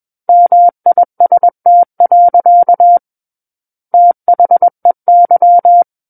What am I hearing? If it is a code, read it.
Morse code, heard as a keyed tone, about 21 wpm, 700 Hz. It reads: MIST. THEY
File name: Book One / 12-Chapter 12 - 21 WPM